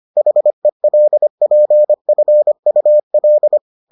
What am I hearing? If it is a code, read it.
Morse code, heard as a keyed tone, about 25 wpm, 600 Hz. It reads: HELPFUL